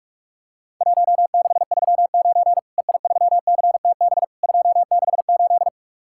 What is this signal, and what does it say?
Morse code, heard as a keyed tone, about 44 wpm, 700 Hz. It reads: 1639 EI3CTB 268
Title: Worked All Europe DX Contest - QTC Exchange - 45wpm